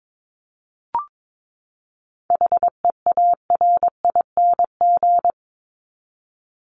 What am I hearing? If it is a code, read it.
Morse code, heard as a keyed tone, about 22 wpm, 700 Hz. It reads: HEARING
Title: Top 1751-2000 Words 22wpm